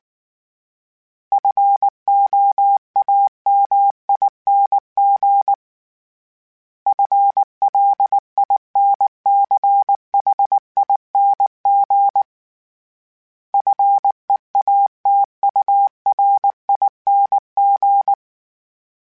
Morse code, heard as a keyed tone, about 19 wpm, 800 Hz. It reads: FOAMING FLINCHING FEATURING
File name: Special INGs / Special INGs-mp3s / Sngl Ing @18